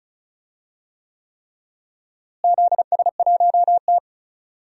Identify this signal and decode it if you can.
Morse code, heard as a keyed tone, about 34 wpm, 700 Hz. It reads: ZS1T
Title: DXCC-100 Call Signs - 4 Characters - Repeated 3x - 35wpm